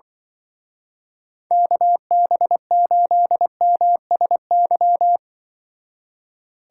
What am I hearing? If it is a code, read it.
Morse code, heard as a keyed tone, about 24 wpm, 700 Hz. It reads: KB8MSY
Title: Speed Racing - USA Call Signs 20wpm